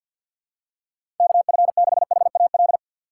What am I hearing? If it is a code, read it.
Morse code, heard as a keyed tone, about 49 wpm, 700 Hz. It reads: KF6HAL
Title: USA Call Signs - 6x Character Spacing - 50wpm